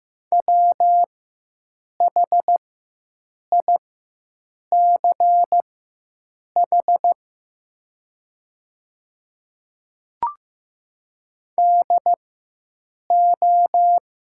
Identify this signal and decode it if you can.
Morse code, heard as a keyed tone, about 15 wpm, spaced as for 7 wpm, 700 Hz. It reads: WHICH DO
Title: 4x Character Spacing - Top 100 Words - 15wpm